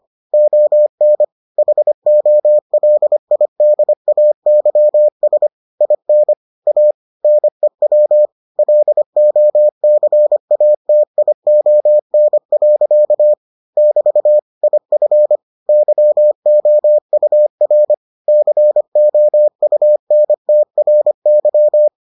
Morse code, heard as a keyed tone, about 25 wpm, 600 Hz. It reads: ON HOLIDAYS IN A NEW LOCATION. <BT> IF YOUR COUNTRY